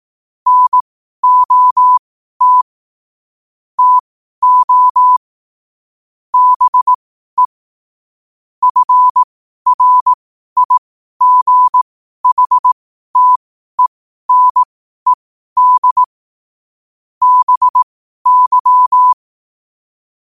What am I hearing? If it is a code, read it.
Morse code, heard as a keyed tone, about 18 wpm, spaced as for 12 wpm, 1000 Hz. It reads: NOT TO BE FRIGHTENED BY
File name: qotdfiles / 12wpm